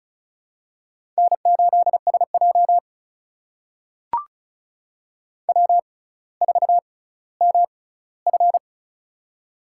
Morse code, heard as a keyed tone, about 36 wpm, spaced as for 18 wpm, 700 Hz. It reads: N8SJ W4MF
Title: USA Call Signs - 4 Characters - 6x Character Spacing - 35wpm